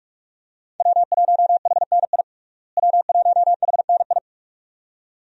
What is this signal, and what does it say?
Morse code, heard as a keyed tone, about 44 wpm, 700 Hz. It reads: W1HNI W1HNI
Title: USA Call Signs - 5 Characters - Repeated 2x - 45wpm